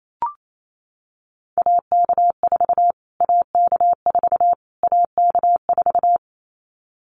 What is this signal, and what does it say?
Morse code, heard as a keyed tone, about 28 wpm, 700 Hz. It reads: AK4 AK4 AK4
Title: Callsign Prefixes for Top 25 DXCC - Repeated 3x - 28wpm